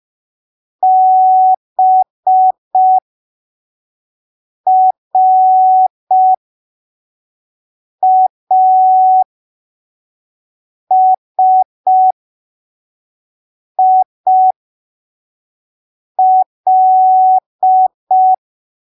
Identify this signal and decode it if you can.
Morse code, heard as a keyed tone, about 5 wpm, spaced as for 3 wpm, 750 Hz. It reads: BRASIL
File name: arquivos / texto / texto 16 05ppm